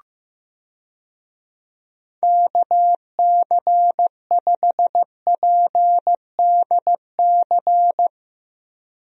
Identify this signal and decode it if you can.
Morse code, heard as a keyed tone, about 16 wpm, 700 Hz. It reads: KC5PDC
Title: USA Call Signs - 6 Characters - 15wpm